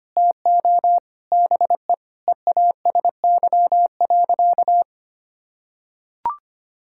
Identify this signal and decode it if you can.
Morse code, heard as a keyed tone, about 25 wpm, 700 Hz. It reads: TO BE EASY.
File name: practice / 25wpm / Sentences with 4-Word Phrases - 25wpm